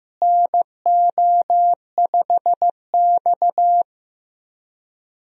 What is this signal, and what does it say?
Morse code, heard as a keyed tone, about 15 wpm, 700 Hz. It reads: NO5X
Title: USA Call Signs - 4 Characters - Repeated 2x - 15wpm